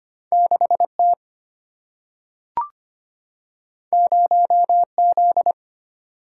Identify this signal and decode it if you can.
Morse code, heard as a keyed tone, about 25 wpm, 700 Hz. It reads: 6 T 0 Z
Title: Single-Letter Single-Number 25wpm